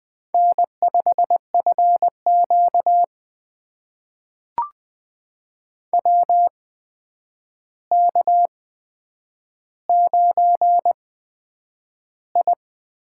Morse code, heard as a keyed tone, about 20 wpm, spaced as for 9 wpm, 700 Hz. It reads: N5FQ WK9I
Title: USA Call Signs - 4 Characters - 8x Character Spacing - 20wpm